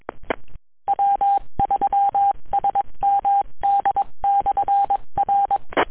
Morse code, heard as a keyed tone, about 22 wpm, 800 Hz. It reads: W3SMD/R